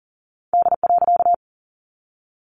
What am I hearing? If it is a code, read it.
Morse code, heard as a keyed tone, about 40 wpm, 700 Hz. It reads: D.
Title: A Sound Of Thunder 40wpm